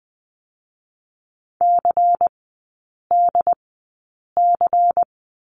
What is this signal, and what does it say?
Morse code, heard as a keyed tone, about 20 wpm, spaced as for 8 wpm, 700 Hz. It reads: CDC
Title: 093 Lesson 16e - Letter c - Three Characters 20wpm